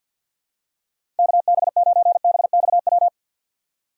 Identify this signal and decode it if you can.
Morse code, heard as a keyed tone, about 46 wpm, 700 Hz. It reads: KB9BXW
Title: USA Call Signs - 3x Character Spacing - 50wpm